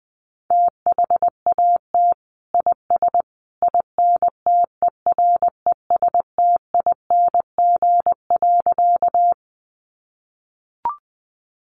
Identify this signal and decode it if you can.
Morse code, heard as a keyed tone, about 20 wpm, 700 Hz. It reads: THAT IS INTERESTING.
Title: Sentences from Top 1000 Words 20wpm